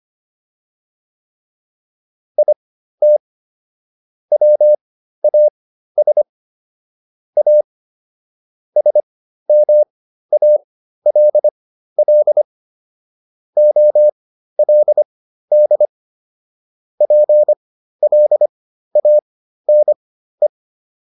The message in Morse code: IT WAS A SMALL OLD PLANE

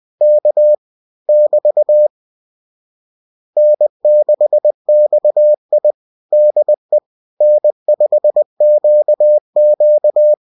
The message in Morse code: K <BT> N6XI DE N5QQ

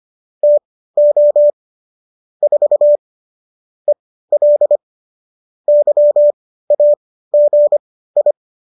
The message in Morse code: TO 4 EL YAGI

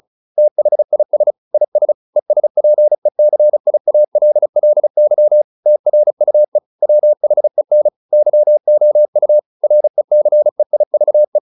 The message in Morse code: THIS IS ESPECIALLY TRUE WHEN YOU RECEIVE

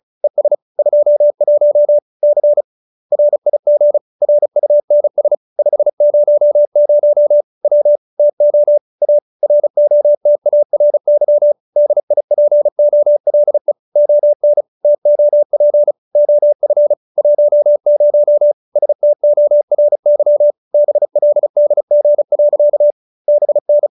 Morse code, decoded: ES 21 C RIG RUNS 500 W TO A ROTARY DIPOLE ON TOP OF 10 STORY BLDG. BN

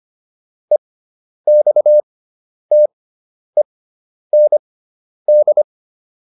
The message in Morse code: EXTEND

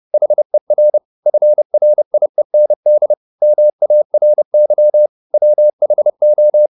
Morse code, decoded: HER FRIEND MARY WHO

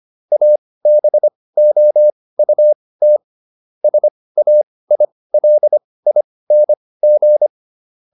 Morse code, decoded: ABOUT SAILING

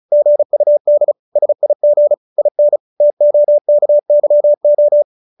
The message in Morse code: GUD SIG IN TOKYO

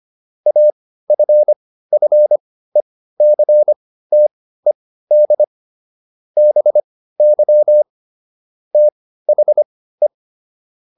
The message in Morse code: AFFECTED BY THE

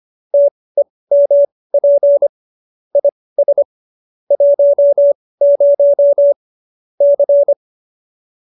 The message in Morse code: TEMP IS 10 C

